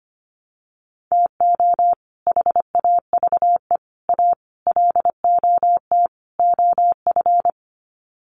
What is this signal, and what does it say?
Morse code, heard as a keyed tone, about 25 wpm, 700 Hz. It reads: TO HAVE A LOT OF
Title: Sets of 5 Words Encore 25wpm